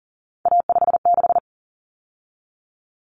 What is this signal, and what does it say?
Morse code, heard as a keyed tone, about 40 wpm, 700 Hz. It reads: A56
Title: Common Cut Numbers - 3-Digit 40wpm